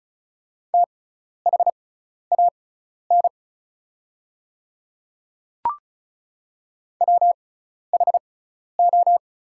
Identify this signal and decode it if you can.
Morse code, heard as a keyed tone, about 36 wpm, spaced as for 12 wpm, 700 Hz. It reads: THAN WHO
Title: 6x Character Spacing - Top 100 Words - 35wpm